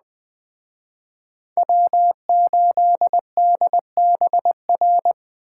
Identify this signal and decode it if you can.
Morse code, heard as a keyed tone, about 20 wpm, 700 Hz. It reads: W8DBR
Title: Speed Racing - USA 1x3 Call Signs 17wpm